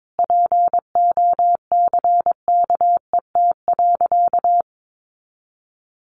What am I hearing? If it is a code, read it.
Morse code, heard as a keyed tone, about 22 wpm, 700 Hz. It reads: POCKET.
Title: A Sound Of Thunder 22wpm